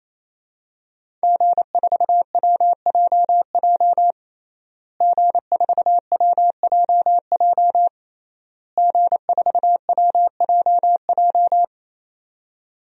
Morse code, heard as a keyed tone, about 28 wpm, 700 Hz. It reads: G4WJJ G4WJJ G4WJJ
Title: DXCC-100 Call Signs - 5 Characters - Repeated 3x - 28wpm